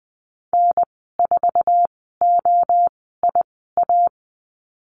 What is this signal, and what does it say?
Morse code, heard as a keyed tone, about 20 wpm, 700 Hz. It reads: N 4 O I A
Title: 076 Lesson 13g - Letter L - Call Signs 20wpm